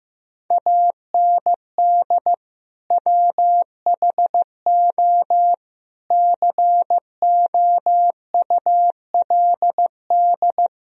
Morse code, heard as a keyed tone, about 15 wpm, 700 Hz. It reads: AND WHO COULD
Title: Sentences from Top 1000 Words - Review Part 3 - 15wpm